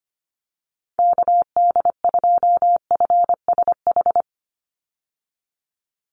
Morse code, heard as a keyed tone, about 25 wpm, 700 Hz. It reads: KD2FSH
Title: Call Signs Repeated 3x - 25wpm